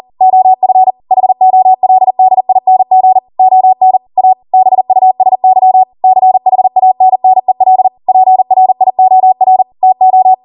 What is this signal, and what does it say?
Morse code, heard as a keyed tone, about 40 wpm, 750 Hz. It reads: OF HOLDING ON A BUSY CHANNEL PRIOR TO
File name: Morse / 40WPM